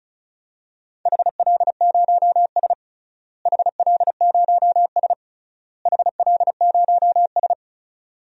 Morse code, heard as a keyed tone, about 34 wpm, 700 Hz. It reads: HL0S HL0S HL0S
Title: DXCC-100 Call Signs - 4 Characters - Repeated 3x - 35wpm